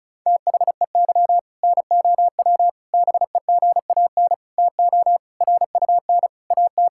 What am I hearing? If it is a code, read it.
Morse code, heard as a keyed tone, about 36 wpm, 700 Hz. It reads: THEY NOW BEGAN TO RUN AT
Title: Sentences from Top 1000 Words - Review Part 5 - 35wpm